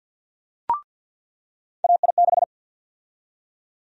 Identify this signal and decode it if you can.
Morse code, heard as a keyed tone, about 53 wpm, 700 Hz. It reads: AI6
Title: USA Call Sign Prefixes - 50wpm